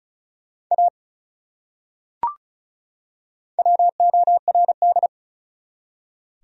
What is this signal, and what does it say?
Morse code, heard as a keyed tone, about 36 wpm, 700 Hz. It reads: A WORD
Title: Top 100 Words 35wpm